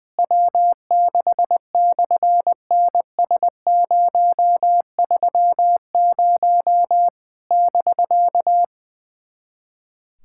Morse code, heard as a keyed tone, about 20 wpm, 700 Hz. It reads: W6/NS030 <BK>
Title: SOTA QSO Practice 20wpm